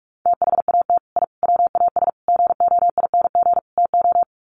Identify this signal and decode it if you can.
Morse code, heard as a keyed tone, about 44 wpm, 700 Hz. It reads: THAT I WAS GOING TO
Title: Sets of 5 Words Encore 45wpm